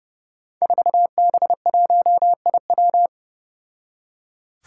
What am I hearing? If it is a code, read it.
Morse code, heard as a keyed tone, about 30 wpm, 700 Hz. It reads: 4B1IW
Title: Five Challenging Letter-Number Combinations 30wpm